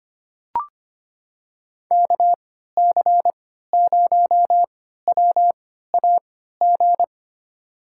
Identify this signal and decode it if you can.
Morse code, heard as a keyed tone, about 25 wpm, spaced as for 14 wpm, 700 Hz. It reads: KC0WAG
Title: USA Call Signs - 6 Characters - 3x Character Spacing - 25wpm